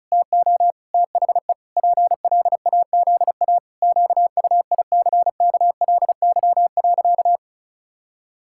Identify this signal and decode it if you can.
Morse code, heard as a keyed tone, about 36 wpm, 700 Hz. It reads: TO THE PLAZA QUICKLY.
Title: Pangram Challenge v2 - 35wpm